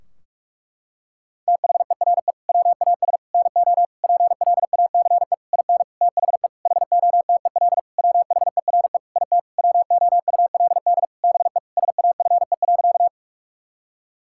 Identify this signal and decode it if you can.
Morse code, heard as a keyed tone, about 46 wpm, 700 Hz. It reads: THERE WAS NO PLACE IN THE HOTEL WHERE IT WOULD BE SAFE.
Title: Sentences from Top 1600 Words 45wpm